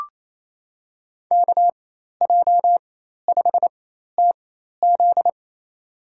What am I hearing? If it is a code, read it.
Morse code, heard as a keyed tone, about 28 wpm, spaced as for 13 wpm, 700 Hz. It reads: KJ5TZ
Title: USA Call Signs - 5 Characters - 4x Character Spacing - 28wpm